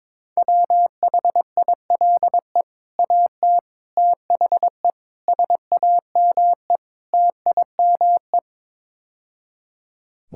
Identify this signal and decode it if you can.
Morse code, heard as a keyed tone, about 22 wpm, 700 Hz. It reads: WHILE AT THE SAME TIME
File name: practice / 22wpm / Build A Phrase - 5-Word Phrases - 22wpm